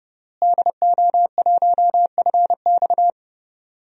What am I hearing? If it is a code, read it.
Morse code, heard as a keyed tone, about 30 wpm, 700 Hz. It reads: DO1FX
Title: DXCC-100 Call Signs - 5 Characters - Repeated 3x - 30wpm